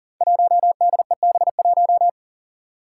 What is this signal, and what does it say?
Morse code, heard as a keyed tone, about 40 wpm, 700 Hz. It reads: 1DEB1
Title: Five Challenging Letter-Number Combinations 40wpm